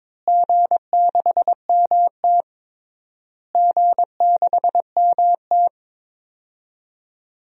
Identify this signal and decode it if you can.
Morse code, heard as a keyed tone, about 22 wpm, 700 Hz. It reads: G6MT G6MT
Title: DXCC-100 Call Signs - 4 Characters - Repeated 3x - 22wpm